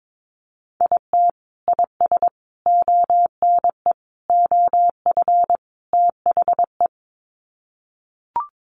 Morse code, heard as a keyed tone, about 22 wpm, 700 Hz. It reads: IT IS ONE OF THE
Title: Sets of 5 Words 22wpm